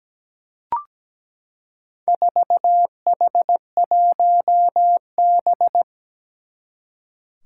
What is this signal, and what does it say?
Morse code, heard as a keyed tone, about 17 wpm, 700 Hz. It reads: 4H1B
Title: Four Challenging Letter-Number Combinations 17wpm